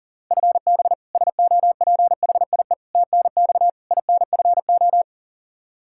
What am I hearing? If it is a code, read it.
Morse code, heard as a keyed tone, about 40 wpm, 700 Hz. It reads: FB SOPHIE TNX INFO
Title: K1USN SST QSO Practice 40wpm